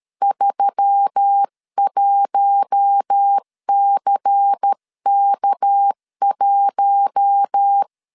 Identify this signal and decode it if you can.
Morse code, heard as a keyed tone, about 13 wpm, 800 Hz. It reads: 31CK1
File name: CW / DET / 1 6 1500C S C 12wpm 3533